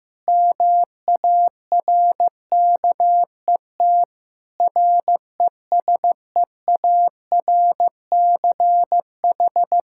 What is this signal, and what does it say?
Morse code, heard as a keyed tone, about 16 wpm, 700 Hz. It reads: MARKET RESEARCH
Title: Practice QSO - Exchange 3 - More Info - 15wpm